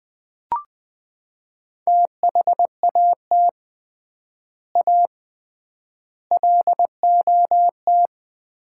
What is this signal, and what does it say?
Morse code, heard as a keyed tone, about 21 wpm, 700 Hz. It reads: THAT A LOT
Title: 3x Word-Spacing - Sets of 4 Words - 20wpm